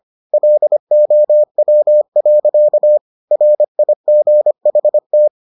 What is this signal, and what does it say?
Morse code, heard as a keyed tone, about 25 wpm, 600 Hz. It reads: LOW. RIGHT